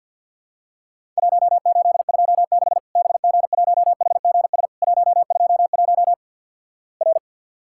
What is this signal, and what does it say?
Morse code, heard as a keyed tone, about 50 wpm, 700 Hz. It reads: 1826 BG1HGS 121 R
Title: Worked All Europe DX Contest - QTC Exchange - 50wpm